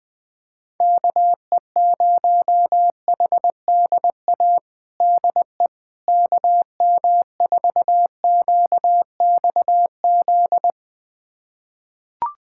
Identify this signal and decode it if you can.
Morse code, heard as a keyed tone, about 21 wpm, 700 Hz. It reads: KE0HDA DE KM4QXZ
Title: USA Callsign DE Callsign - 6 Characters - 20wpm